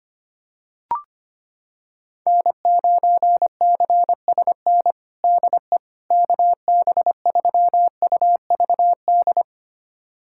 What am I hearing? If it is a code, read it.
Morse code, heard as a keyed tone, about 25 wpm, 700 Hz. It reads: N9CSN DE KB3UVD
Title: USA Callsign DE Callsign 25wpm